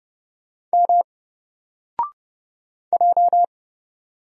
Morse code, heard as a keyed tone, about 31 wpm, 700 Hz. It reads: M J
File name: Single-Letter-Number-30wpm